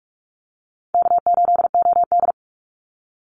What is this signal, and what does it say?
Morse code, heard as a keyed tone, about 46 wpm, 700 Hz. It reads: K8OD
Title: Easy Call Signs - 45wpm